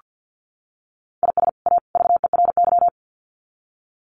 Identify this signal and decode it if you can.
Morse code, heard as a keyed tone, about 48 wpm, 700 Hz. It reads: IS A VERY
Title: Sets of 3 Words 50wpm